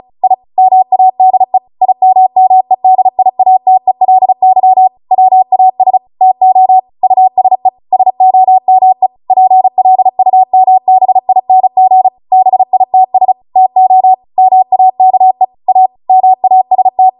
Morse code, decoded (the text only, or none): I MADE IMMEDIATELY WAS TO USE SOME PLUMBING BITS TO MAKE A MAST